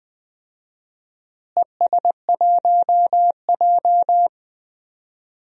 Morse code, heard as a keyed tone, about 20 wpm, 690 Hz. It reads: ES1J
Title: DXCC-100 Call Signs - 4 Characters - Repeated 3x - 20wpm